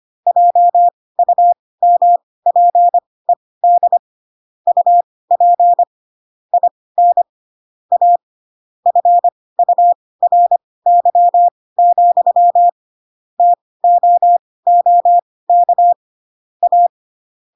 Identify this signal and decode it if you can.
Morse code, heard as a keyed tone, about 25 wpm, spaced as for 18 wpm, 700 Hz. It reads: JUMPED UP IN A FURY, TOOK A
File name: Pinocchio / Pinocchio ch4 12 25 ik1hge